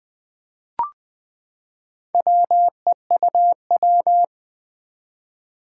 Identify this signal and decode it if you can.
Morse code, heard as a keyed tone, about 20 wpm, 700 Hz. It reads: WEUW
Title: Four Challenging Letter-Number Combinations 20wpm